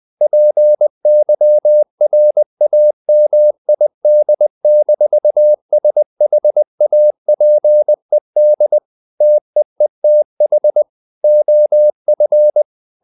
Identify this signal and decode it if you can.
Morse code, heard as a keyed tone, about 20 wpm, 600 Hz. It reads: PYRAMID-SHAPED TEETH OF
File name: the old man and the sea part 2 20wpm